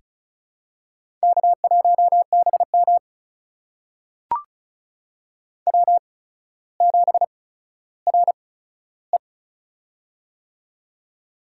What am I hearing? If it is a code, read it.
Morse code, heard as a keyed tone, about 36 wpm, spaced as for 15 wpm, 700 Hz. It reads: K1BM W7RE
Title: USA Call Signs - 4 Characters - 8x Character Spacing - 35wpm